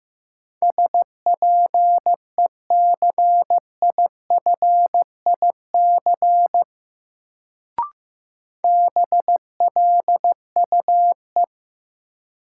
Morse code, heard as a keyed tone, about 15 wpm, 700 Hz. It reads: SPECIFIC BLUE
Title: Sentences from Top 1000 Words - Review Part 3 - New Word Drill - 15wpm